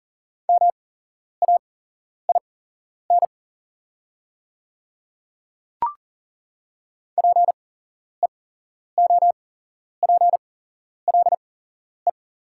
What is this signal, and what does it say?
Morse code, heard as a keyed tone, about 40 wpm, spaced as for 11 wpm, 690 Hz. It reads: MAIN PEOPLE
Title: 8x Character Spacing - Top 300 Words - 40wpm